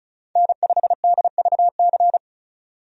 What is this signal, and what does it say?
Morse code, heard as a keyed tone, about 36 wpm, 700 Hz. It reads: N5DVC
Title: USA Call Signs - 5 Characters - 35wpm